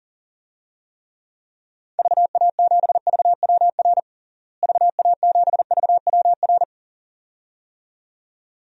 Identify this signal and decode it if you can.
Morse code, heard as a keyed tone, about 40 wpm, 700 Hz. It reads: VA7VWR VA7VWR
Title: DXCC-100 Call Signs - 6 Characters - Repeated 2x - 40wpm